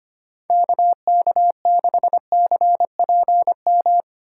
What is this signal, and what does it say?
Morse code, heard as a keyed tone, about 25 wpm, 700 Hz. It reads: KK6CPM
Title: USA Call Signs - 6 Characters - 4x Character Spacing - 25wpm